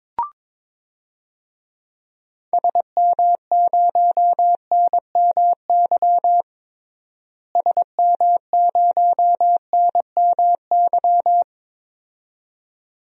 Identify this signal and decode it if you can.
Morse code, heard as a keyed tone, about 22 wpm, 700 Hz. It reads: SM0NMY SM0NMY
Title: DXCC-100 Call Signs - 6 Characters - Repeated 2x - 22wpm